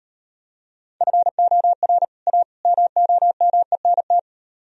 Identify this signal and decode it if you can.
Morse code, heard as a keyed tone, about 38 wpm, 700 Hz. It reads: FOR A MOMENT